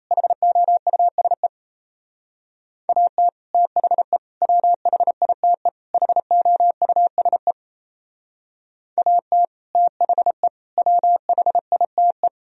Morse code, heard as a keyed tone, about 33 wpm, 700 Hz. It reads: HOUSE AT THE WHITE HOUSE AT THE WHITE
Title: Speed-Racing - Sets of 4 Words 28wpm